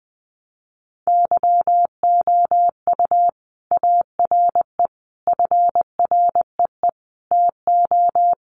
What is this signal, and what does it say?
Morse code, heard as a keyed tone, about 20 wpm, 700 Hz. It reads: YOU ARE FREE TO
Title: Sentences from Top 700 Words 20wpm